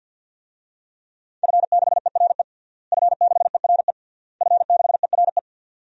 Morse code, heard as a keyed tone, about 51 wpm, 700 Hz. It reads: F6ERE F6ERE F6ERE
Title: DXCC-100 Call Signs - 5 Characters - Repeated 3x - 50wpm